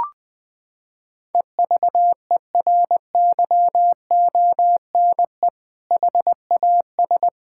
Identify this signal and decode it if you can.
Morse code, heard as a keyed tone, about 20 wpm, 700 Hz. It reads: EVERYONE HAS
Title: Sentences from Top 1000 Words - Review Part 5 - 20wpm